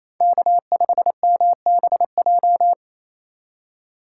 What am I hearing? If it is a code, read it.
Morse code, heard as a keyed tone, about 28 wpm, 700 Hz. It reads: K5MBJ
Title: USA Call Signs - 5 Characters - 4x Character Spacing - 28wpm